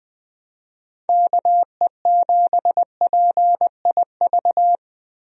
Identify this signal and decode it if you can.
Morse code, heard as a keyed tone, about 21 wpm, 700 Hz. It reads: KE7PIV
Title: USA Call Signs - 6 Characters - Repeated 3x - 20wpm